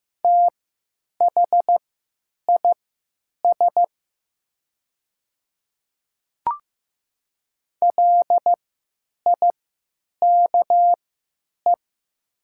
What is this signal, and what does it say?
Morse code, heard as a keyed tone, about 15 wpm, spaced as for 9 wpm, 690 Hz. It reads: THIS LIKE
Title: 3x Character Spacing - Top 100 Words - 15wpm